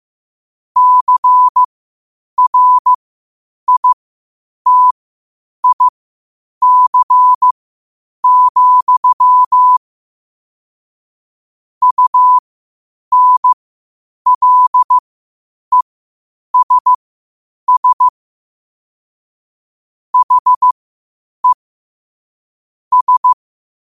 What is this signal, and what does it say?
Morse code, heard as a keyed tone, about 15 wpm, spaced as for 8 wpm, 1000 Hz. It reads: CRITIC, UNLESS HE S
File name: qotdfiles / 8wpm